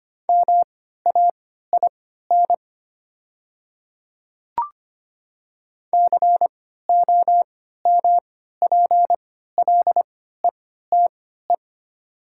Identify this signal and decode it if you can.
Morse code, heard as a keyed tone, about 25 wpm, spaced as for 14 wpm, 700 Hz. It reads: MAIN COMPLETE
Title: 3x Character Spacing - Top 500 Words - 25wpm